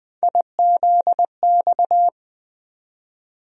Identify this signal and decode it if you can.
Morse code, heard as a keyed tone, about 20 wpm, 700 Hz. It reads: IZX
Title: USA Call Sign Postfixes - 3 Letters 20wpm